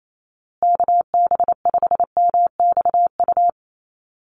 Morse code, heard as a keyed tone, about 28 wpm, 700 Hz. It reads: KB5MXU
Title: USA Call Signs Repeated 2x 28wpm